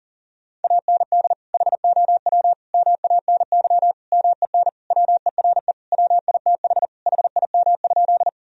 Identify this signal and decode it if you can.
Morse code, heard as a keyed tone, about 40 wpm, 700 Hz. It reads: AND HOW MANY MEN WERE WITH HIM?
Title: Sentences from Top 1000 Words - Review Part 1 - 40wpm